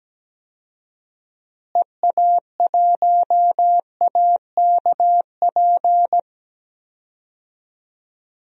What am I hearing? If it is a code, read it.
Morse code, heard as a keyed tone, about 17 wpm, 700 Hz. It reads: EA1AKP
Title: DXCC-100 Call Signs - 6 Characters - Repeated 3x - 17wpm